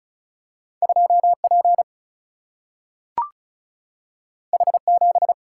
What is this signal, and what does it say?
Morse code, heard as a keyed tone, about 36 wpm, 700 Hz. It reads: 2 P H 7
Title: Single-Letter Single-Number 35wpm